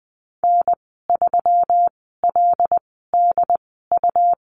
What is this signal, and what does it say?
Morse code, heard as a keyed tone, about 20 wpm, spaced as for 15 wpm, 700 Hz. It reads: N3LDU
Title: 122 Lesson 20g - Number 3 - Call Signs 20wpm